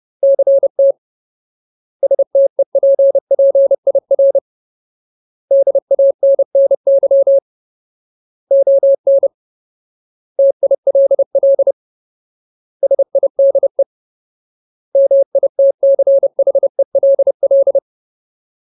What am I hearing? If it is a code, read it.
Morse code, heard as a keyed tone, about 30 wpm, 550 Hz. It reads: CT STEPPIR DANNY ON TILL SIDE MITCHELL
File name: Ultimate Word List / 30WPM Ultimate